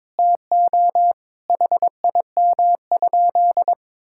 Morse code, heard as a keyed tone, about 22 wpm, 700 Hz. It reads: TO HIM?
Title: Sentences from Top 1000 Words - Review Part 6 - 22wpm